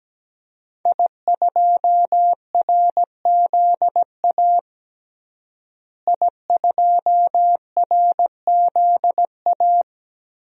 Morse code, heard as a keyed tone, about 17 wpm, 700 Hz. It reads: I2RZA I2RZA
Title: DXCC-100 Call Signs - 5 Characters - Repeated 2x - 17wpm